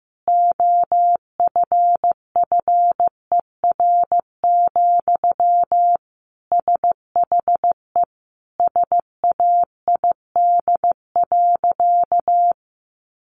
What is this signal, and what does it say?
Morse code, heard as a keyed tone, about 15 wpm, 700 Hz. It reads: OFFER, SHE SAID.
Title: Sentences from Top 1300 Words 15wpm